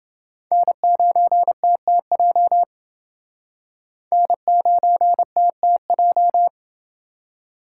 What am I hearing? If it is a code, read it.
Morse code, heard as a keyed tone, about 28 wpm, 700 Hz. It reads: N9TTJ N9TTJ